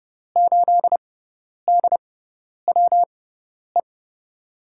Morse code, heard as a keyed tone, about 29 wpm, spaced as for 10 wpm, 700 Hz. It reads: 8DWE
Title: USA Call Signs - 5 Characters - 6x Character Spacing - 30wpm